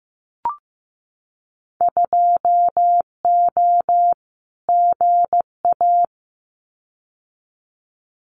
Morse code, heard as a keyed tone, about 15 wpm, 700 Hz. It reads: 2O GA
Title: Winter Field Day Exchanges Encore 15wpm